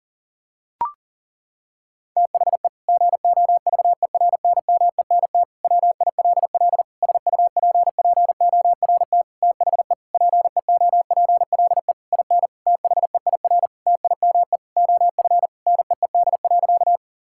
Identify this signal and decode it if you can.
Morse code, heard as a keyed tone, about 40 wpm, 700 Hz. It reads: THE GOVERNMENT WILL SUPPORT THE PEOPLE IN THEIR TIME OF NEED.
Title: Sentences from Top 1000 Words - Review Part 5 - 40wpm